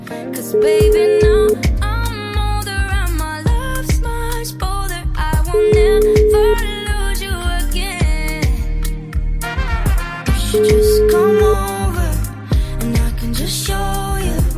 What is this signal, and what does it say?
Ringback tone (Europe, 425 Hz), with noise.